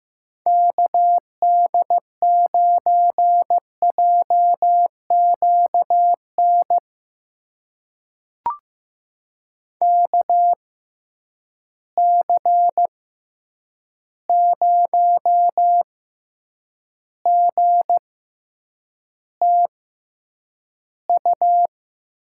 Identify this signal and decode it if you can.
Morse code, heard as a keyed tone, about 15 wpm, spaced as for 8 wpm, 700 Hz. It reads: KD9JQN KC0GTU
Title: USA Call Signs - 6 Characters - 6x Character Spacing - 15wpm